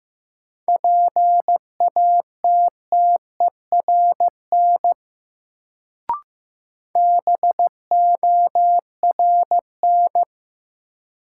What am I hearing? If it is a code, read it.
Morse code, heard as a keyed tone, about 15 wpm, 700 Hz. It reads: PATTERN BORN
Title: Sentences from Top 1000 Words - New Word Drill - 15wpm